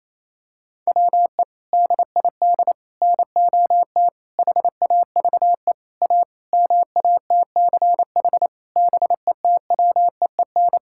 Morse code, heard as a keyed tone, about 28 wpm, 700 Hz. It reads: WE DID NOT HAVE A MATCH BETWEEN
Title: Sentences from Top 1600 Words 28wpm